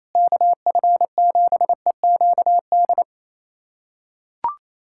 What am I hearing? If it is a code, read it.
Morse code, heard as a keyed tone, about 27 wpm, 700 Hz. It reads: KF7EQD
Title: USA Callsign DE Callsign - 6 Characters - 28wpm